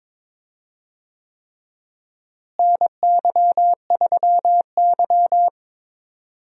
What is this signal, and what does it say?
Morse code, heard as a keyed tone, about 22 wpm, 700 Hz. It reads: NY3Y
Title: USA Call Signs - 4 Characters - Repeated 3x - 22wpm